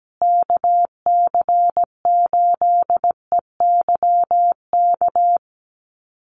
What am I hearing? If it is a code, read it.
Morse code, heard as a keyed tone, about 17 wpm, 700 Hz. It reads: KC8EYK
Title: USA Call Signs Repeated 2x 17wpm